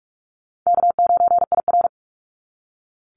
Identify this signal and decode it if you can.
Morse code, heard as a keyed tone, about 46 wpm, 700 Hz. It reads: K9IR
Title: NA SOTA Activator Callsigns 45wpm